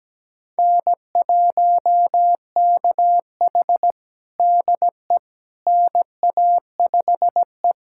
Morse code, heard as a keyed tone, about 17 wpm, 700 Hz. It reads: N1KH DE NA5E
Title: USA Callsign DE Callsign - 4 Characters - 17wpm